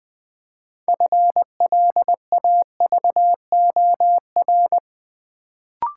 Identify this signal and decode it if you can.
Morse code, heard as a keyed tone, about 20 wpm, 700 Hz. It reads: FLAVOR
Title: Build A Word - 3 Letters At A Time - 6-Letter Words - 20wpm